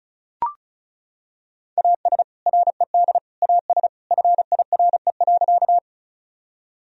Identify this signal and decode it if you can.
Morse code, heard as a keyed tone, about 34 wpm, 700 Hz. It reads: AS RED AS FIRE.
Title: Sentences from Top 1000 Words - Review Part 2 - 35wpm